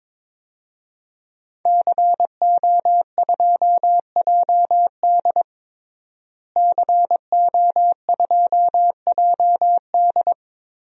Morse code, heard as a keyed tone, about 22 wpm, 700 Hz. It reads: CO2JD CO2JD
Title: DXCC-100 Call Signs - 5 Characters - Repeated 3x - 22wpm